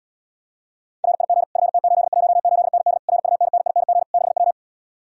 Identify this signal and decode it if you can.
Morse code, heard as a keyed tone, about 77 wpm, 700 Hz. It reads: RIG FT818ND RUNNING 6W